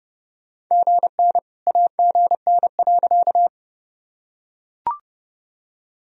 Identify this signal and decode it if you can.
Morse code, heard as a keyed tone, about 30 wpm, 700 Hz. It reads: GN AGN.
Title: Sentences from Top 300 Words and Abbreviations 30wpm